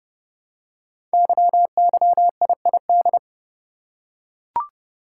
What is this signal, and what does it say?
Morse code, heard as a keyed tone, about 28 wpm, 700 Hz. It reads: YYIID
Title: Code Groups - 5 Letter:Numbers - 30wpm